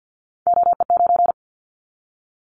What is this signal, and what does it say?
Morse code, heard as a keyed tone, about 50 wpm, 700 Hz. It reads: OE9
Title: Callsign Prefixes for Top 25 DXCC - 50wpm